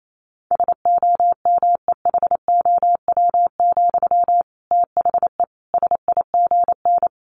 Morse code, heard as a keyed tone, about 28 wpm, 700 Hz. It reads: SOMEHOW, THE SIGN
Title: A Sound Of Thunder 28wpm